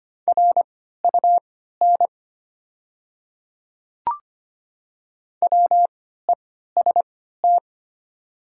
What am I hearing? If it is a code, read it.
Morse code, heard as a keyed tone, about 25 wpm, spaced as for 14 wpm, 700 Hz. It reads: RUN WEST
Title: 3x Character Spacing - Top 500 Words - 25wpm